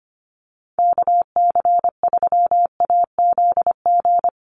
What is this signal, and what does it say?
Morse code, heard as a keyed tone, about 25 wpm, 700 Hz. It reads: KC3AZG
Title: USA Call Signs Repeated 2x 25wpm